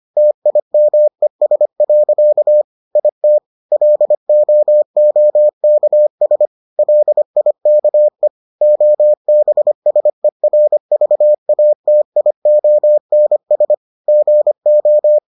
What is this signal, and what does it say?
Morse code, heard as a keyed tone, about 25 wpm, 600 Hz. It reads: TIMES. IT LOOKS LIKE OBSERVATIONS GO